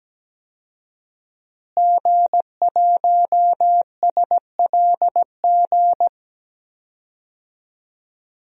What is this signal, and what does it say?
Morse code, heard as a keyed tone, about 17 wpm, 700 Hz. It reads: G1SLG
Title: DXCC-100 Call Signs - 5 Characters - Repeated 2x - 17wpm